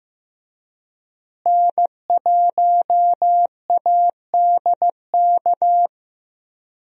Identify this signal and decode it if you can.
Morse code, heard as a keyed tone, about 16 wpm, 700 Hz. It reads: N1ADK
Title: USA Call Signs - 5 Characters - Repeated 3x - 15wpm